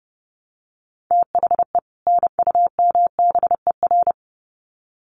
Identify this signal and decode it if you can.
Morse code, heard as a keyed tone, about 30 wpm, 700 Hz. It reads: THE NUMBER
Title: Sets of 2 Words 30wpm